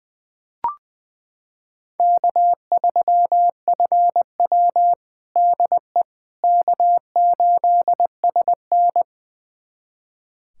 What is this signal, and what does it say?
Morse code, heard as a keyed tone, about 21 wpm, 700 Hz. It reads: K3FW DE K8SN
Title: USA Callsign DE Callsign - 4 Characters - 20wpm